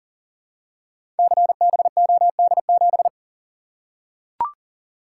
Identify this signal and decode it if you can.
Morse code, heard as a keyed tone, about 39 wpm, 700 Hz. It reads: CBOD7
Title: Code Groups - 5 Letter:Numbers - 40wpm